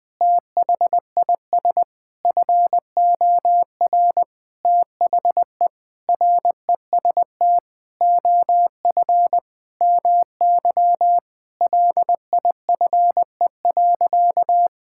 Morse code, heard as a keyed tone, about 20 wpm, 700 Hz. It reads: THIS FOR THE REST OF MY LIFE.
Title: Sentences with 5-Word Phrases - 20wpm